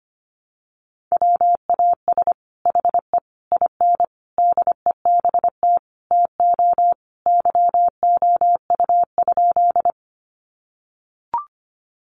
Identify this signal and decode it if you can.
Morse code, heard as a keyed tone, about 25 wpm, 700 Hz. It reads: WAS HE IN DEBT TO YOU?
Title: Sentences from Top 1300 Words 25wpm